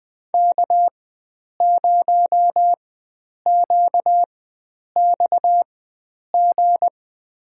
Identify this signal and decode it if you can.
Morse code, heard as a keyed tone, about 21 wpm, spaced as for 9 wpm, 700 Hz. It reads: K0QXG
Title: USA Call Signs - 5 Characters - 4x Character Spacing - 20wpm